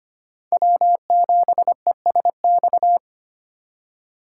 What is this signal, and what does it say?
Morse code, heard as a keyed tone, about 25 wpm, 700 Hz. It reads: W7ESX
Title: USA Call Signs - 5 Characters - 8x Character Spacing - 25wpm